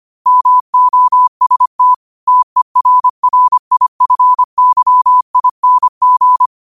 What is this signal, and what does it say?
Morse code, heard as a keyed tone, about 25 wpm, 1000 Hz. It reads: MOST TERRIFYING
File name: qotdfiles / 25wpm